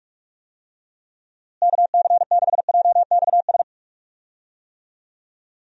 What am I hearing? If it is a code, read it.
Morse code, heard as a keyed tone, about 46 wpm, 700 Hz. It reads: KC6JXS
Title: USA Call Signs - 6 Characters - Repeated 2x - 45wpm